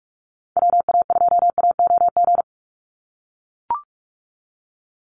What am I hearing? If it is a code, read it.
Morse code, heard as a keyed tone, about 44 wpm, 700 Hz. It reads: WA2AOG
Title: Call Signs - 45wpm